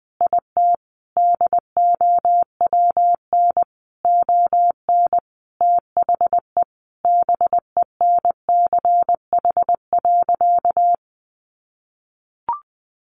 Morse code, heard as a keyed tone, about 20 wpm, 700 Hz. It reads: IT DOWN ON THE BENCH.
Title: Sentences from Top 1400 Words 20wpm